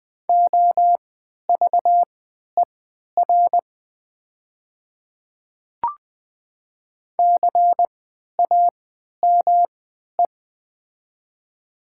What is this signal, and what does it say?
Morse code, heard as a keyed tone, about 20 wpm, spaced as for 11 wpm, 700 Hz. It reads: OVER CAME
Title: 3x Character Spacing - Top 300 Words - 20wpm